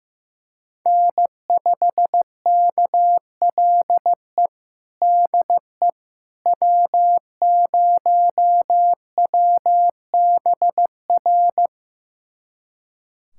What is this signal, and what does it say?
Morse code, heard as a keyed tone, about 15 wpm, 700 Hz. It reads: N5KLE DE W0WBR
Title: USA Callsign DE Callsign - 5 Characters - 15wpm